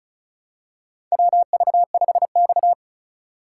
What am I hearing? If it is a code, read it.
Morse code, heard as a keyed tone, about 36 wpm, 700 Hz. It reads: WV5X
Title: USA Call Signs - 4x Character Spacing - 35wpm